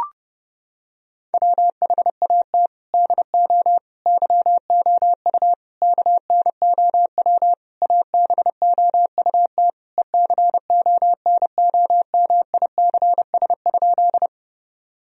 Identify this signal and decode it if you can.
Morse code, heard as a keyed tone, about 30 wpm, 700 Hz. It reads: WHAT DO YOU KNOW ABOUT ECONOMICS?
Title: Sentences from Top 1000 Words - Review Part 3 - 30wpm